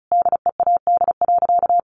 Morse code, heard as a keyed tone, about 34 wpm, 700 Hz. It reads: DEAD.
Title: Sentences from Top 1500 Words 35wpm